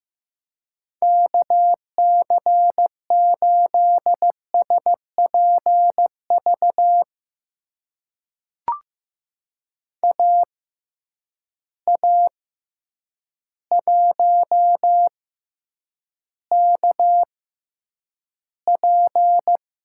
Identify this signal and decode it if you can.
Morse code, heard as a keyed tone, about 15 wpm, spaced as for 8 wpm, 700 Hz. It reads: KC8SPV AA1KP
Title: USA Call Signs - 6x Character Spacing - 15wpm